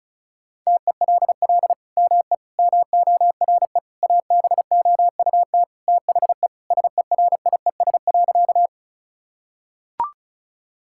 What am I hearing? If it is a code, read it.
Morse code, heard as a keyed tone, about 36 wpm, 700 Hz. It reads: TELL ME MORE ABOUT THE SERIES.
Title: Sentences from Top 1000 Words - Review Part 5 - 35wpm